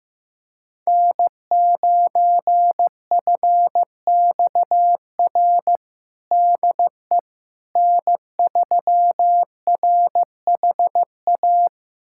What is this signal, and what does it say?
Morse code, heard as a keyed tone, about 16 wpm, 700 Hz. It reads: N9FXR DE N3RHA
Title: USA Callsign DE Callsign - 5 Characters - 15wpm